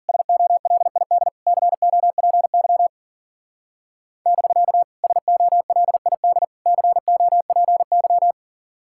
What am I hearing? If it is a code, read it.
Morse code, heard as a keyed tone, about 49 wpm, 700 Hz. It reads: SOLID COPY <BK> SOLID COPY
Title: Speed-Racing Common QSO Phrases - 40wpm